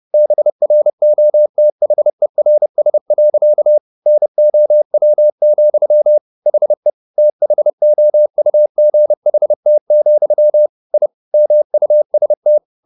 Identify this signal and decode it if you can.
Morse code, heard as a keyed tone, about 30 wpm, 600 Hz. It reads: BROTHERS. NOW, HE THOUGHT, I MUST